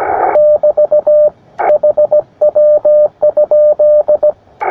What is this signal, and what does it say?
Morse code, heard as a keyed tone, about 17 wpm, 600 Hz, with radio noise. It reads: <BT> HW?
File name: attachments / sp4ipl 20m